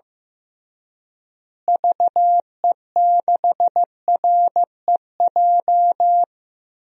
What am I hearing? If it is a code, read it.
Morse code, heard as a keyed tone, about 15 wpm, 700 Hz. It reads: VE6REJ
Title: DXCC-100 Call Signs - 6 Characters - Repeated 3x - 15wpm